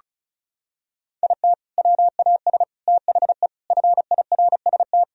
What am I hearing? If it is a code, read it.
Morse code, heard as a keyed tone, about 34 wpm, 700 Hz. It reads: IT WAS THE FIRST
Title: Sentences from Top 1000 Words - Review Part 1 - 35wpm